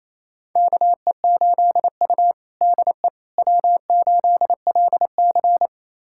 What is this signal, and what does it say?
Morse code, heard as a keyed tone, about 28 wpm, 700 Hz. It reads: KE8U DE W8LC
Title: USA Callsign DE Callsign - 4 Characters - 28wpm